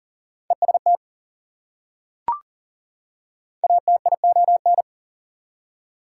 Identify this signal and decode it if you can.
Morse code, heard as a keyed tone, about 40 wpm, 700 Hz. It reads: EST ATION
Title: Prefixes and Suffixes 40wpm